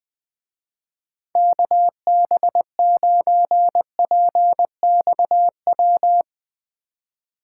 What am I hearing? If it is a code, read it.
Morse code, heard as a keyed tone, about 20 wpm, 700 Hz. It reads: KB9PXW